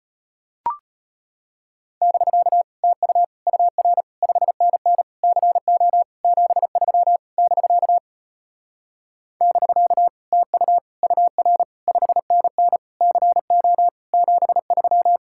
Morse code, heard as a keyed tone, about 33 wpm, 700 Hz. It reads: <BK> TU UR 5NN CO 73 <BK> <BK> TU UR 5NN CO 73